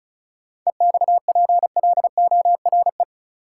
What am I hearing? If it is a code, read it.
Morse code, heard as a keyed tone, about 36 wpm, 700 Hz. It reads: EXPLORE
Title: Sentences from Top 1100 Words - New Word Drill Alt - 35wpm